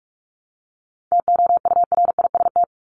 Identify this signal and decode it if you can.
Morse code, heard as a keyed tone, about 44 wpm, 700 Hz. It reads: TOURIST
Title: Top 1501-1750 Words 45wpm